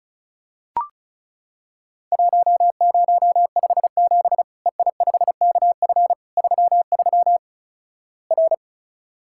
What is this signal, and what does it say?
Morse code, heard as a keyed tone, about 34 wpm, 700 Hz. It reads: 1057 EI5KF 33 R
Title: Worked All Europe DX Contest - QTC Exchange - 35wpm